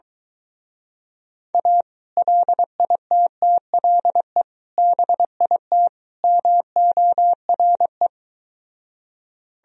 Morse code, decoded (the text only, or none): A LITTLE BIT MORE